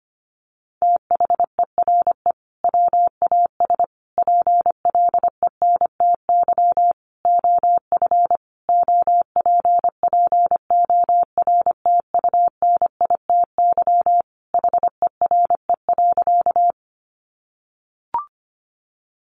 Morse code, heard as a keyed tone, about 25 wpm, 700 Hz. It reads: THERE WAS PLENTY OF OPPORTUNITY HERE.
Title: Sentences from Top 1300 Words 25wpm